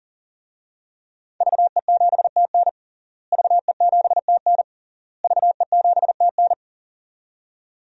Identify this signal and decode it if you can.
Morse code, heard as a keyed tone, about 37 wpm, 700 Hz. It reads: VE7TN VE7TN VE7TN
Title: DXCC-100 Call Signs - 5 Characters - Repeated 3x - 40wpm